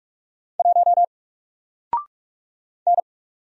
Morse code, heard as a keyed tone, about 44 wpm, 700 Hz. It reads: 1 N
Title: Single Letter-Number 45wpm